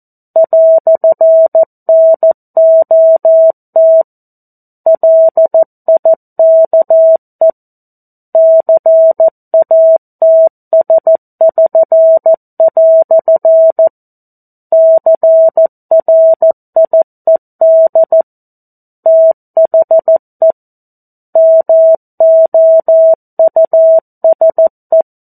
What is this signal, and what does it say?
Morse code, heard as a keyed tone, about 14 wpm, 650 Hz. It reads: "NOT LIKE CATS<SN>" CRIED THE MOUSE
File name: 02-Chapter-02 14 WPM